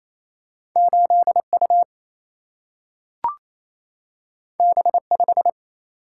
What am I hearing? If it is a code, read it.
Morse code, heard as a keyed tone, about 28 wpm, 700 Hz. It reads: 8U B5
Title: Single-Letter Single-Number 28wpm